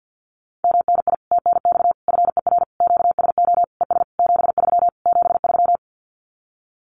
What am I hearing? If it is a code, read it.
Morse code, heard as a keyed tone, about 50 wpm, 700 Hz. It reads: MNI TNX FER QSO ES 73 73
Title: Common QSO Phrases Encore 50wpm